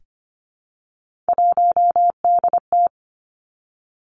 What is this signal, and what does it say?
Morse code, heard as a keyed tone, about 25 wpm, 700 Hz. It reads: 1DT
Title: Three Challenging Letter-Number Combinations 25wpm